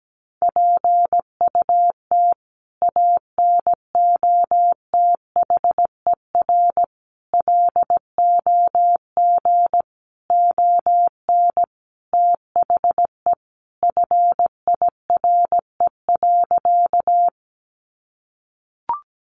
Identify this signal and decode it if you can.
Morse code, heard as a keyed tone, about 17 wpm, 700 Hz. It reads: PUT ANOTHER LOG ON THE FIRE.
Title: Sentences from Top 1400 Words 17wpm